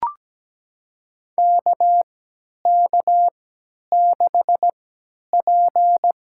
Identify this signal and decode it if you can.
Morse code, heard as a keyed tone, about 17 wpm, spaced as for 10 wpm, 700 Hz. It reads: KK6P
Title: USA Call Signs - 4 Characters - 3x Character Spacing - 17wpm